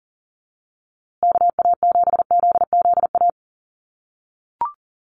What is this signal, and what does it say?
Morse code, heard as a keyed tone, about 40 wpm, 700 Hz. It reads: KA7ZZA
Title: USA Call Signs Repeated 3x 40wpm